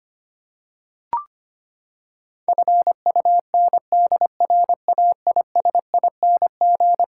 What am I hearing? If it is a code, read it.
Morse code, heard as a keyed tone, about 25 wpm, 700 Hz. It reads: FUNDRAISING
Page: morse-code-ninja-practice-25wpm